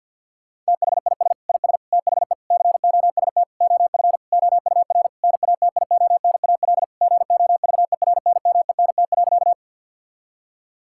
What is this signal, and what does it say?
Morse code, heard as a keyed tone, about 52 wpm, 690 Hz. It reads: THIS IS THE COST OF OUR NATIONAL GOVERNMENT.
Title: Sentences from Top 1000 Words - Review Part 5 - 50wpm